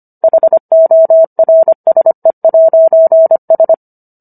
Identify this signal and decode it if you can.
Morse code, heard as a keyed tone, about 25 wpm, 650 Hz. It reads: HORSE'S